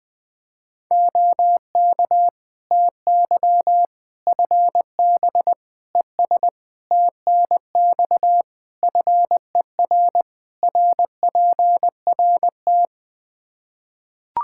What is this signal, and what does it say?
Morse code, heard as a keyed tone, about 20 wpm, 700 Hz. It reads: OK TY FB ES TNX FER RPRT
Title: K1USN SST QSO Practice 20wpm